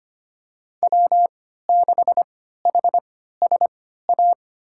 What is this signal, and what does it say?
Morse code, heard as a keyed tone, about 25 wpm, spaced as for 14 wpm, 700 Hz. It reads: W6HSA
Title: USA Call Signs - 5 Characters - 3x Character Spacing - 25wpm